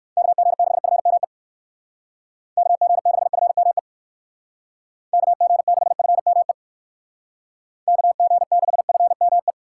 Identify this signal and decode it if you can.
Morse code, heard as a keyed tone, about 58 wpm, 700 Hz. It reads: KG6FME KG6FME KG6FME KG6FME